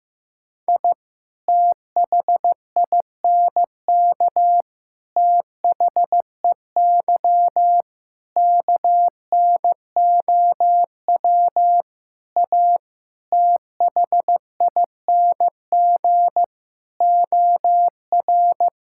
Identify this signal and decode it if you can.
Morse code, heard as a keyed tone, about 15 wpm, 700 Hz. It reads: I THINK THEY KNOW A THING OR
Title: Sentences from Top 1000 Words - Review Part 5 - 15wpm